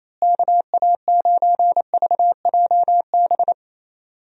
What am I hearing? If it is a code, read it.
Morse code, heard as a keyed tone, about 28 wpm, 700 Hz. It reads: KA9VJB
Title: USA Call Signs - 6 Characters - Repeated 2x - 28wpm